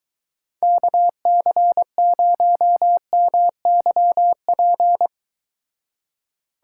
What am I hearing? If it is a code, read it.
Morse code, heard as a keyed tone, about 23 wpm, 700 Hz. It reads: KC0MYP